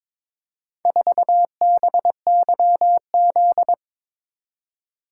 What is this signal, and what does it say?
Morse code, heard as a keyed tone, about 23 wpm, 700 Hz. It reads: 4BYZ
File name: practice / 22wpm / Four Letter-Number Combinations - 22wpm